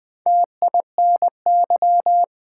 Morse code, heard as a keyed tone, about 20 wpm, 700 Hz. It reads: TINY
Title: Four Letter Words - 20wpm